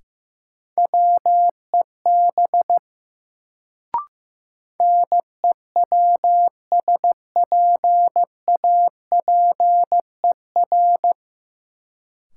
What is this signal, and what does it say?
Morse code, heard as a keyed tone, about 15 wpm, 700 Hz. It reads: WEB NEWSPAPER
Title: Sentences from Top 1000 Words - Review Part 4 - New Word Drill - 15wpm